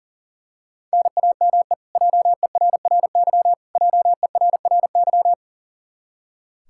Morse code, heard as a keyed tone, about 40 wpm, 700 Hz. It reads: NAME JERRY JERRY
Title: Practice QSO - Exchange 1 - Beginning - 40wpm